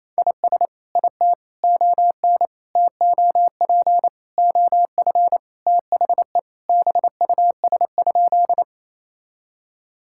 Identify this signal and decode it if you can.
Morse code, heard as a keyed tone, about 27 wpm, 700 Hz. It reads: IS IT ON TOP OF THE BUS?
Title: Sentences from Top 1000 Words - Review Part 6 - 28wpm